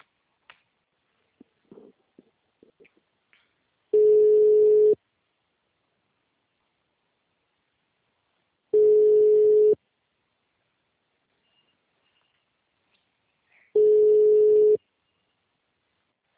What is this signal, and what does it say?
Ringback tone (Europe, 425 Hz).
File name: out- -2001- - - .65856